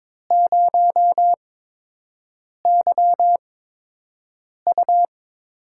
Morse code, heard as a keyed tone, about 22 wpm, spaced as for 6 wpm, 700 Hz. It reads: 0YU
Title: USA Call Signs - 8x Character Spacing - 22wpm